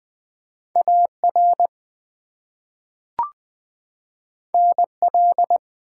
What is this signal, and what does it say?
Morse code, heard as a keyed tone, about 20 wpm, 700 Hz. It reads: AR NL
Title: ARRL Sections v2 - 20wpm